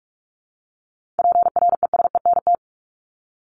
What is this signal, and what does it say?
Morse code, heard as a keyed tone, about 44 wpm, 700 Hz. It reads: PRESENT
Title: Top 501-750 Words 45wpm